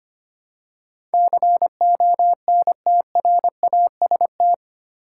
Morse code, heard as a keyed tone, about 25 wpm, 700 Hz. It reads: CONTRAST
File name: 8 Letter Words - First 3 Letters Given - 25wpm